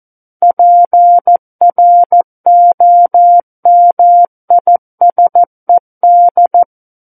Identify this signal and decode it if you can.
Morse code, heard as a keyed tone, about 14 wpm, 700 Hz. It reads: PROMISED
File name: Book One / 05-Chapter 5 - 14 WPM